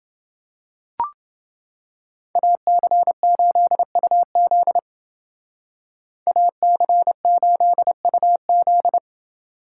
Morse code, heard as a keyed tone, about 25 wpm, 700 Hz. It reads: AC8UZ AC8UZ